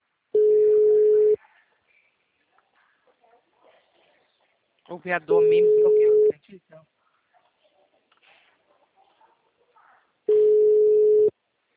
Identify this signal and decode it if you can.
Ringback tone (Europe, 425 Hz), with speech.